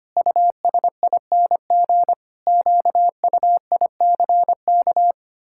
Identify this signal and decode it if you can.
Morse code, heard as a keyed tone, about 25 wpm, 700 Hz. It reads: USING QUICK
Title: Pangram Challenge v2 - 25wpm